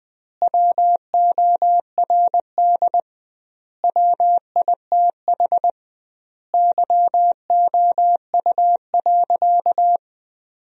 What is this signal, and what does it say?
Morse code, heard as a keyed tone, about 20 wpm, 700 Hz. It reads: WORD WITH YOU.
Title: 2x Word-Spacing - Sentences from Top 300 Words - 20wpm